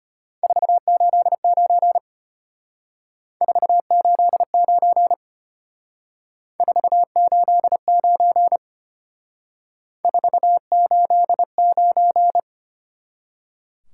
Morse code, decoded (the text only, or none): 489 489 489 489